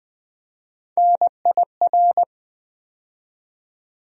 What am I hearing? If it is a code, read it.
Morse code, heard as a keyed tone, about 20 wpm, 700 Hz. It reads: NIR
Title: USA Call Sign Postfixes - 3 Letters 20wpm